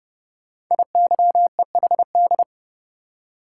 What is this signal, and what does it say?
Morse code, heard as a keyed tone, about 30 wpm, 700 Hz. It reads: IYEHD
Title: Code Groups - 5 Letters - 30wpm